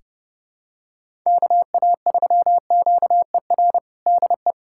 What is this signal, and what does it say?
Morse code, heard as a keyed tone, about 28 wpm, 700 Hz. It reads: KA3QER DE
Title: USA Callsign DE Callsign - 6 Characters - 30wpm